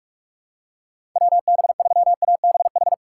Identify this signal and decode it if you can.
Morse code, heard as a keyed tone, about 46 wpm, 700 Hz. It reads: WB3ABH
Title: USA Call Signs - 6 Characters - 2x Character Spacing - 45wpm